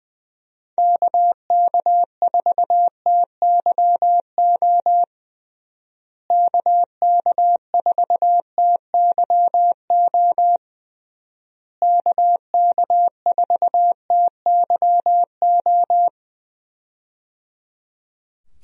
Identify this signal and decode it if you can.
Morse code, heard as a keyed tone, about 20 wpm, 700 Hz. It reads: KK4TYO KK4TYO KK4TYO
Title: USA Call Signs - 6 Characters - Repeated 3x - 20wpm